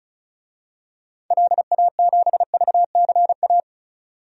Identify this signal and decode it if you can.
Morse code, heard as a keyed tone, about 34 wpm, 700 Hz. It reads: LA7VCA
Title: DXCC-100 Call Signs - 6 Characters - 35wpm